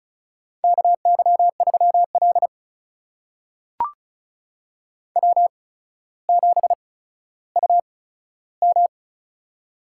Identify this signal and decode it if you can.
Morse code, heard as a keyed tone, about 36 wpm, spaced as for 15 wpm, 700 Hz. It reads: KY3L W7UM
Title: USA Call Signs - 4 Characters - 8x Character Spacing - 35wpm